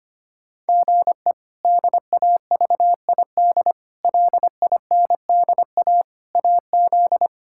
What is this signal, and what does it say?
Morse code, heard as a keyed tone, about 25 wpm, 700 Hz. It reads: GE DAVID LINDA AZ
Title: K1USN SST QSO Practice - 25wpm